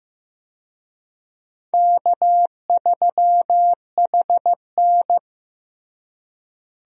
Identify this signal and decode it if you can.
Morse code, heard as a keyed tone, about 15 wpm, 700 Hz. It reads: K3HN
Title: USA Call Signs - 4 Characters - Repeated 3x - 15wpm